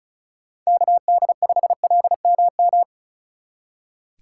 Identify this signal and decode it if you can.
Morse code, heard as a keyed tone, about 44 wpm, 700 Hz. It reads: KD5LMM